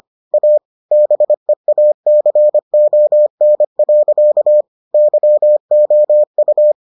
Morse code, heard as a keyed tone, about 25 wpm, 600 Hz. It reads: A BEACON. YOU